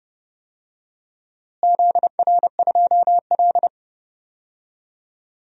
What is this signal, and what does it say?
Morse code, heard as a keyed tone, about 30 wpm, 700 Hz. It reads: ZR2L
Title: DXCC-100 Call Signs - 4 Characters - Repeated 3x - 30wpm